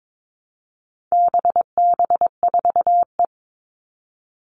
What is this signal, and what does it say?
Morse code, heard as a keyed tone, about 22 wpm, 700 Hz. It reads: BB4E
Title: All Cut Numbers - 4-Digit 22wpm